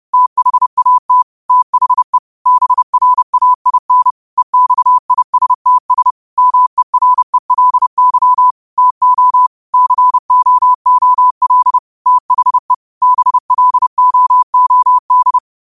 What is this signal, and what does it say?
Morse code, heard as a keyed tone, about 30 wpm, 1000 Hz. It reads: THAT THE BRAIN EXISTS MERELY TO COOL THE BLOOD
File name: qotdfiles / 30wpm